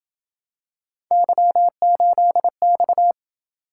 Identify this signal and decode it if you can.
Morse code, heard as a keyed tone, about 28 wpm, 700 Hz. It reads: Y8X